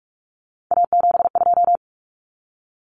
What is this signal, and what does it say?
Morse code, heard as a keyed tone, about 44 wpm, 700 Hz. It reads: A72
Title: Common Cut Numbers - 3-Digit 45wpm